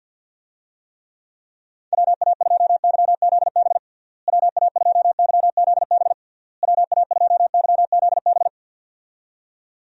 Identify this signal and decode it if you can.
Morse code, heard as a keyed tone, about 51 wpm, 700 Hz. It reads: WA2YZB WA2YZB WA2YZB
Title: USA Call Signs - 6 Characters - Repeated 3x - 50wpm